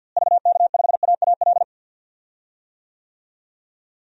Morse code, heard as a keyed tone, about 53 wpm, 700 Hz. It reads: VK5AAL
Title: DXCC-100 Call Signs - 6 Characters - Repeated 2x - 50wpm